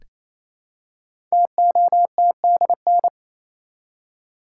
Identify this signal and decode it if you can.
Morse code, heard as a keyed tone, about 29 wpm, 700 Hz. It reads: TOTDN
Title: Code Groups - 5 Letters - 28wpm